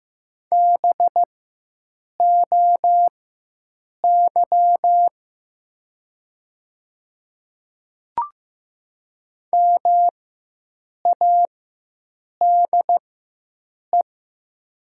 Morse code, heard as a keyed tone, about 15 wpm, spaced as for 7 wpm, 700 Hz. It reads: BOY MADE
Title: 4x Character Spacing - Top 200 Words - 15wpm